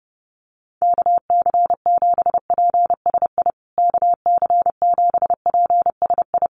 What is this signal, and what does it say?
Morse code, heard as a keyed tone, about 30 wpm, 700 Hz. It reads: KC7PSI KC7PSI
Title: USA Call Signs Repeated 3x 30wpm